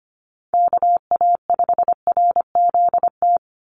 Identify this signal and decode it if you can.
Morse code, heard as a keyed tone, about 25 wpm, 700 Hz. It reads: KA5RZT
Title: USA Call Signs 25wpm